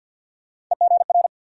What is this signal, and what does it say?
Morse code, heard as a keyed tone, about 47 wpm, 700 Hz. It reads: EGR
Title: USA Call Sign Postfixes - 3 Letters 50wpm